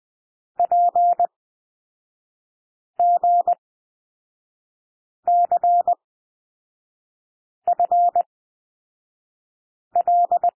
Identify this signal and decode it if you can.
Morse code, heard as a keyed tone, about 20 wpm, spaced as for 5 wpm, 700 Hz. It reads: PGCFL
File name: SOUND LETTERS 20 wpm